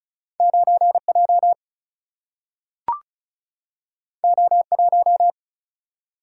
Morse code, heard as a keyed tone, about 36 wpm, 700 Hz. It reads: 9J O1
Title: Single-Letter Single-Number 35wpm